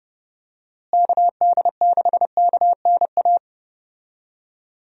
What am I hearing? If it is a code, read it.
Morse code, heard as a keyed tone, about 31 wpm, 700 Hz. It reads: KD6KNA
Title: USA Call Signs - 6 Characters - Repeated 2x - 30wpm